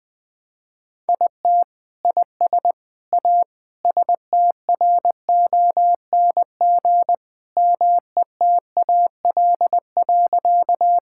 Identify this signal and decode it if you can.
Morse code, heard as a keyed tone, about 21 wpm, 700 Hz. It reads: IT IS A STRONG METAL.
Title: Sentences from Top 1000 Words - Review Part 6 - 20wpm